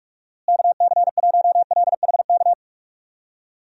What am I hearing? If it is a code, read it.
Morse code, heard as a keyed tone, about 47 wpm, 700 Hz. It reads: KC1LHK
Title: USA Call Signs - 6 Characters - Repeated 3x - 45wpm